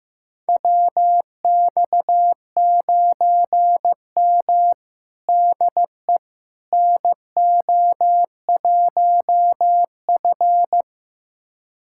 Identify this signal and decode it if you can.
Morse code, heard as a keyed tone, about 15 wpm, 700 Hz. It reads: WX9M DE NO1F
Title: USA Callsign DE Callsign - 4 Characters - 15wpm